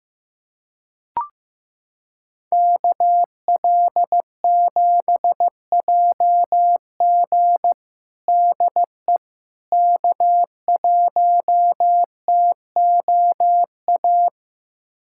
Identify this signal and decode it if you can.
Morse code, heard as a keyed tone, about 15 wpm, 700 Hz. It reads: KL7JG DE K1TOA
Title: USA Callsign DE Callsign - 5 Characters - 15wpm